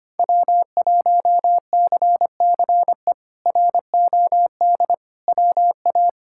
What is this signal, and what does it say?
Morse code, heard as a keyed tone, about 25 wpm, 700 Hz. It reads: W1CCE ROD WA
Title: K1USN SST QSO Practice - 25wpm